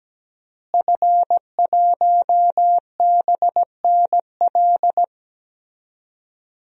Morse code, heard as a keyed tone, about 17 wpm, 700 Hz. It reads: F1BNL
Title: DXCC-100 Call Signs - 5 Characters - Repeated 2x - 17wpm